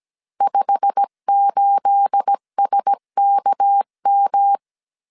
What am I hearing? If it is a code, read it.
Morse code, heard as a keyed tone, about 17 wpm, 800 Hz. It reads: 58SKM
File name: 1500C S B 16wpm